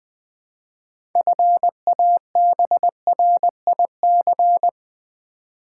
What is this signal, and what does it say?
Morse code, heard as a keyed tone, about 20 wpm, 700 Hz. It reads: FABRIC
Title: 6 Letter Words - First 3 Letters Given - 20wpm